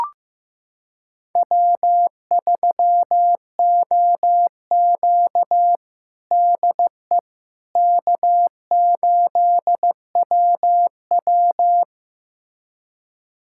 Morse code, heard as a keyed tone, about 15 wpm, 700 Hz. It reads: W3OQ DE K8WW
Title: USA Callsign DE Callsign - 4 Characters - 15wpm